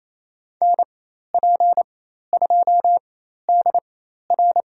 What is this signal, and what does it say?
Morse code, heard as a keyed tone, about 28 wpm, spaced as for 13 wpm, 700 Hz. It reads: NP2DR
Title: USA Call Signs - 4x Character Spacing - 28wpm